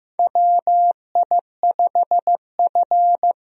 Morse code, heard as a keyed tone, about 15 wpm, 700 Hz. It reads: WI5F
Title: USA Call Signs - 4 Characters - Repeated 3x - 15wpm